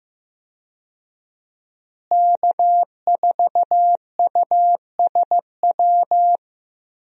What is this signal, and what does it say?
Morse code, heard as a keyed tone, about 15 wpm, 700 Hz. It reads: K4USW
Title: USA Call Signs - 5 Characters - Repeated 3x - 15wpm